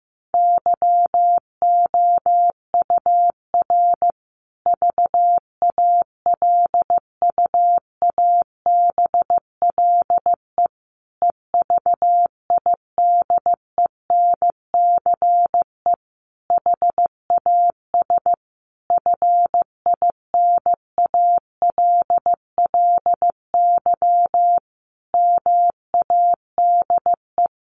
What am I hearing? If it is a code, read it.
Morse code, heard as a keyed tone, about 15 wpm, 700 Hz. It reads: YOUR VALUABLE EVIDENCE HAS FINALLY MADE
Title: Sentences from Top 1300 Words 15wpm